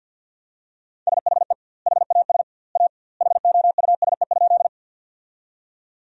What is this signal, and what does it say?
Morse code, heard as a keyed tone, about 50 wpm, 700 Hz. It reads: SHE HAS A HOUSE?
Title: 2x Word-Spacing - Sentences from Top 200 Words - 50wpm